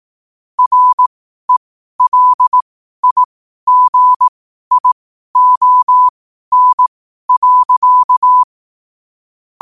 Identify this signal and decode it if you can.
Morse code, heard as a keyed tone, about 18 wpm, spaced as for 12 wpm, 1000 Hz. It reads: RELIGION.
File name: qotdfiles / 12wpm